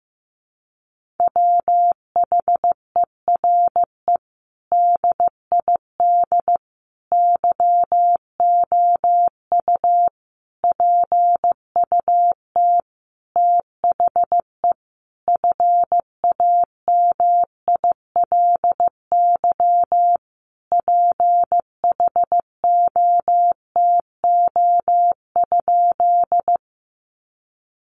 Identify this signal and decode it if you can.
Morse code, heard as a keyed tone, about 15 wpm, 700 Hz. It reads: WHERE DID YOU PUT THE FAMILY PHOTO?
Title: Sentences from Top 1200 Words 15wpm